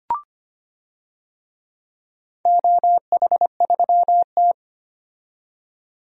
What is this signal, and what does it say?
Morse code, heard as a keyed tone, about 25 wpm, 700 Hz. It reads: OH3T
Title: DXCC-100 Call Signs - 4 Characters - 25wpm